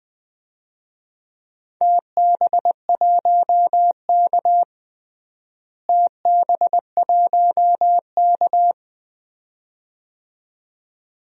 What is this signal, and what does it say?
Morse code, heard as a keyed tone, about 21 wpm, 700 Hz. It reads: TB1K TB1K
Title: DXCC-100 Call Signs - 4 Characters - Repeated 2x - 20wpm